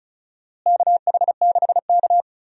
Morse code, heard as a keyed tone, about 36 wpm, 700 Hz. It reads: KH6K
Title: DXCC-100 Call Signs - 4 Characters - Repeated 3x - 35wpm